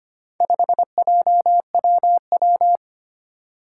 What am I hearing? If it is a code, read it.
Morse code, heard as a keyed tone, about 25 wpm, 700 Hz. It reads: 5JWW
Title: Four Challenging Letter-Number Combinations 25wpm